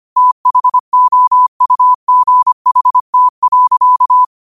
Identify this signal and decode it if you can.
Morse code, heard as a keyed tone, about 25 wpm, 1000 Hz. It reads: THOUGHT.
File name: 25wpm